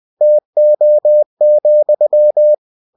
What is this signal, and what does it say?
Morse code, heard as a keyed tone, about 20 wpm, 600 Hz. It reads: TO,
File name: cw-books / winnie-the-pooh / winnie-20wpm-page-0096